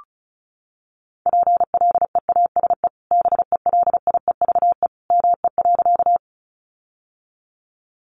Morse code, heard as a keyed tone, about 34 wpm, 700 Hz. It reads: PLEASE BELIEVE ME.
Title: Sentences from Top 1500 Words 35wpm